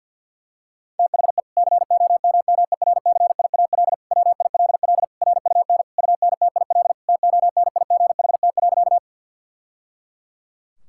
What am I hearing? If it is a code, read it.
Morse code, heard as a keyed tone, about 50 wpm, 700 Hz. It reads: THE COMMERCIAL WILL RUN UNTIL TONIGHT.
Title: Sentences from Top 1000 Words - Review Part 5 - 50wpm